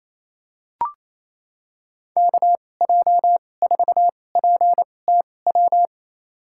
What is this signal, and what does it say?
Morse code, heard as a keyed tone, about 28 wpm, spaced as for 20 wpm, 700 Hz. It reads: KJ4PTW
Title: USA Call Signs - 2x Character Spacing - 28wpm